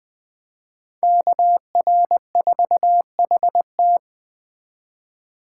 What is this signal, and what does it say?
Morse code, heard as a keyed tone, about 20 wpm, 700 Hz. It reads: KR4HT
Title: Speed Racing - USA 2x2 Call Signs 17wpm